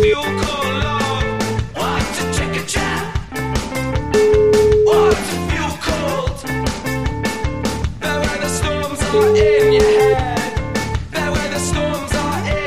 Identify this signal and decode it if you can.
Ringback tone (Europe, 425 Hz), with noise.